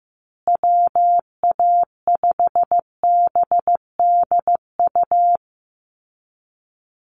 Morse code, heard as a keyed tone, about 14 wpm, 700 Hz. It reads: WA5BDU
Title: Call Signs Repeated 3x - 15wpm